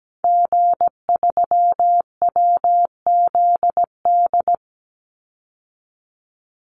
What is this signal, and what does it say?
Morse code, heard as a keyed tone, about 17 wpm, 700 Hz. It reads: G3WZD
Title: Call Signs Repeated 2x - 17wpm